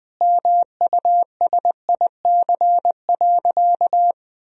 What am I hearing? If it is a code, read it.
Morse code, heard as a keyed tone, about 20 wpm, 700 Hz. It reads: MUSIC.
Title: Sentences from Top 1000 Words - Review Part 1 - 20wpm